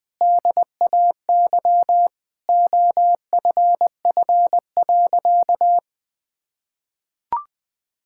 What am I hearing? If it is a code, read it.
Morse code, heard as a keyed tone, about 20 wpm, 700 Hz. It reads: DAY OFF.
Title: Sentences with 2-Word Phrases - 20wpm